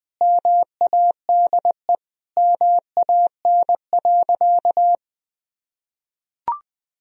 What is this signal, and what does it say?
Morse code, heard as a keyed tone, about 21 wpm, 700 Hz. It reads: MADE MAN.
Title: Sentences from Top 1000 Words - Review Part 1 - 20wpm